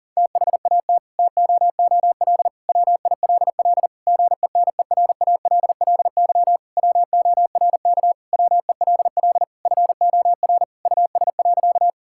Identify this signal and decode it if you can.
Morse code, heard as a keyed tone, about 40 wpm, 700 Hz. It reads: THAT TOOL WILL GENERALLY WORK WELL FOR US.
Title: Sentences from Top 1000 Words - Review Part 6 - 40wpm